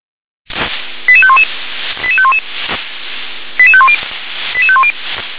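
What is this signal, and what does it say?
Five-tone selective call (ZVEI), heard as quick runs of five tones, tones of about 70 ms. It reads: 88511 88511 88510 88510
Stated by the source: ZVEI five-tone call received on SDR.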